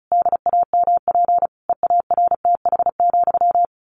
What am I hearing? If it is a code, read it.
Morse code, heard as a keyed tone, about 34 wpm, 700 Hz. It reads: DAMP EARTH,
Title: A Sound Of Thunder 35wpm